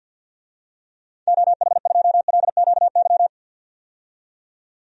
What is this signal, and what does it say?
Morse code, heard as a keyed tone, about 50 wpm, 700 Hz. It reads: OH2LQY
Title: DXCC-100 Call Signs - 6 Characters - 50wpm